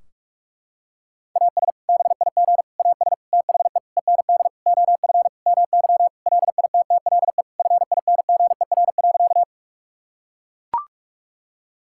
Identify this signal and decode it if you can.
Morse code, heard as a keyed tone, about 41 wpm, 700 Hz. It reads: AS BIG AS THE END OF MY LITTLE FINGER.
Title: Sentences from Top 1600 Words 45wpm